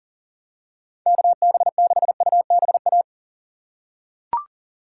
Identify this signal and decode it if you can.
Morse code, heard as a keyed tone, about 38 wpm, 700 Hz. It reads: KB6UBA
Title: USA Call Signs - 6 Characters - 2x Character Spacing - 40wpm